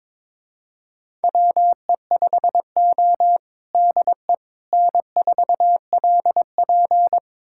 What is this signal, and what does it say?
Morse code, heard as a keyed tone, about 22 wpm, 700 Hz. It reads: WE5O DE N4LP
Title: USA Callsign DE Callsign - 4 Characters - 22wpm